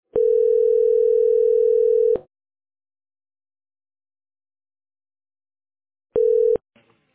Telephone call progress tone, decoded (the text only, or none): ringback tone